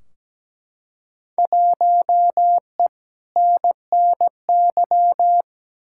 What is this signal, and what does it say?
Morse code, heard as a keyed tone, about 17 wpm, 700 Hz. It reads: 1E NNY
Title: Field Day Exchanges - 17wpm